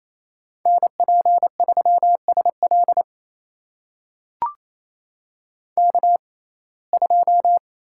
Morse code, heard as a keyed tone, about 28 wpm, spaced as for 20 wpm, 700 Hz. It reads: NP3SL K2
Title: USA Call Signs - 5 Characters - 6x Character Spacing - 28wpm